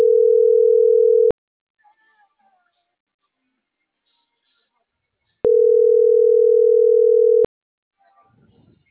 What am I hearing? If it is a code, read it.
Ringback tone (North America), 440 + 480 Hz.